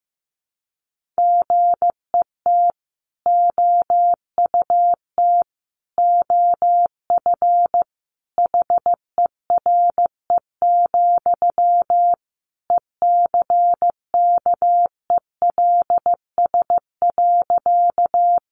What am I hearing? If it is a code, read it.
Morse code, heard as a keyed tone, about 15 wpm, 700 Hz. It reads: GET OUT OF HERE, ECKELS.
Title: A Sound Of Thunder 15wpm